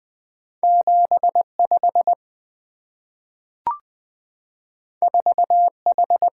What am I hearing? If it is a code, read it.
Morse code, heard as a keyed tone, about 20 wpm, 700 Hz. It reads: 75 4H
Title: Two Letter-Number Combinations - 20wpm